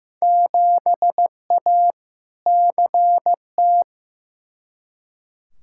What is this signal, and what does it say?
Morse code, heard as a keyed tone, about 16 wpm, 700 Hz. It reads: 7A CT
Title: Field Day Exchanges - 15wpm